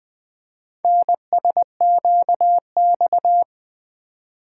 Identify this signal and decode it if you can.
Morse code, heard as a keyed tone, about 20 wpm, 700 Hz. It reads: NSQX
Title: Four Letter-Number Combinations - 20wpm